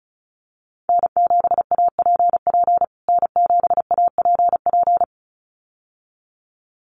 Morse code, heard as a keyed tone, about 34 wpm, 700 Hz. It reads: N7APP N7APP
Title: USA Call Signs Repeated 2x 35wpm